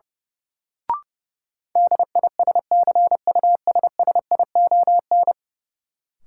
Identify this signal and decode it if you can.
Morse code, heard as a keyed tone, about 30 wpm, 700 Hz. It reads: DISCUSSION
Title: Sentences from Top 1100 Words - New Word Drill - 30wpm